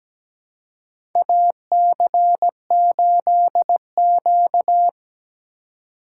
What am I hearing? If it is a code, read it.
Morse code, heard as a keyed tone, about 17 wpm, 700 Hz. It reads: AC8Q
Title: USA Call Signs - 4x Character Spacing - 17wpm